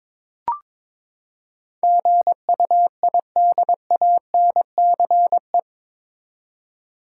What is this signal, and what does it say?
Morse code, heard as a keyed tone, about 22 wpm, 700 Hz. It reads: GUIDANCE
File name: practice / 22wpm / Eight Letter Words - 22wpm